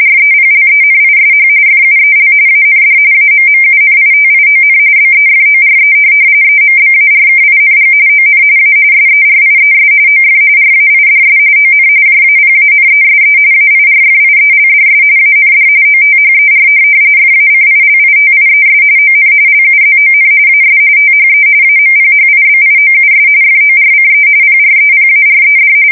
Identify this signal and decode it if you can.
RTTY, about 45.45 baud. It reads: REPLENISH FUEL AND AMMUNITION BEFORE EMBARKING ON A SERIES OF RAIDS ON JAPANESE AIRFIELDS AND OTHER FACILITIES ON LUZON AS THE AMPHIBIOUS FORCE PREPARED
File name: fsk170 / USS-Massachusetts